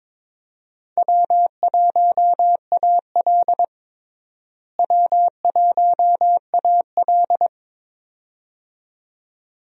Morse code, heard as a keyed tone, about 22 wpm, 700 Hz. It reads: W1AL W1AL
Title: USA Call Signs - 4 Characters - Repeated 2x - 22wpm